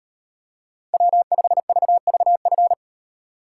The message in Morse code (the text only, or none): W5VVF